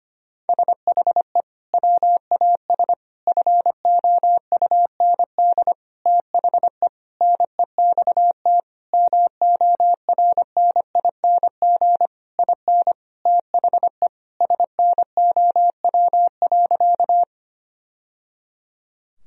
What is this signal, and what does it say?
Morse code, heard as a keyed tone, about 25 wpm, 700 Hz. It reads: SHE WAS FOUND THE NEXT MORNING IN THE SNOW.
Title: Sentences from Top 1000 Words - Review Part 3 - 25wpm